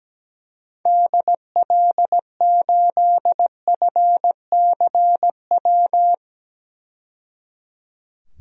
Morse code, heard as a keyed tone, about 17 wpm, 700 Hz. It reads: DL8FCW
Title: DXCC-100 Call Signs - 6 Characters - 17wpm